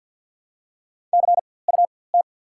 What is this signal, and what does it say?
Morse code, heard as a keyed tone, about 54 wpm, spaced as for 24 wpm, 700 Hz. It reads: CUT
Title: 4x Character Spacing - Top 300 Words - 50wpm